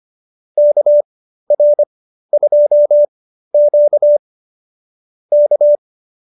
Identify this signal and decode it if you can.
Morse code, heard as a keyed tone, about 25 wpm, spaced as for 13 wpm, 590 Hz. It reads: KR2Q K